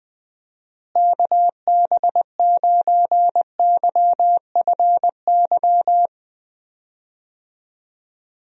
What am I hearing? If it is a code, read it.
Morse code, heard as a keyed tone, about 20 wpm, 700 Hz. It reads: KB9YFY
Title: USA Call Signs - 6 Characters - Repeated 3x - 20wpm